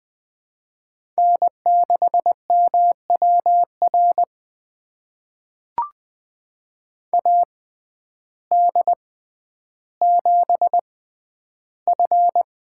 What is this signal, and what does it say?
Morse code, heard as a keyed tone, about 20 wpm, spaced as for 11 wpm, 700 Hz. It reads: N6MWR AD7F
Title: USA Call Signs - 5 Characters - 6x Character Spacing - 20wpm